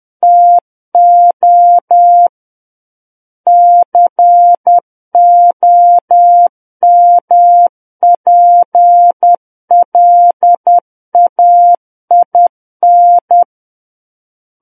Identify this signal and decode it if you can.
Morse code, heard as a keyed tone, about 10 wpm, 700 Hz. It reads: TO COMPLAIN
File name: Book One / 01-Chapter 1 - 10 WPM